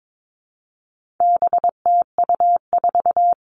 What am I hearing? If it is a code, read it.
Morse code, heard as a keyed tone, about 22 wpm, 700 Hz. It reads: BTU4
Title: All Cut Numbers - 4-Digit 22wpm